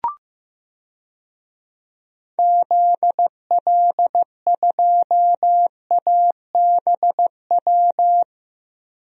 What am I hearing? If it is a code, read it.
Morse code, heard as a keyed tone, about 15 wpm, 700 Hz. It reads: ZL2ABW
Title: DXCC-100 Call Signs - 6 Characters - 15wpm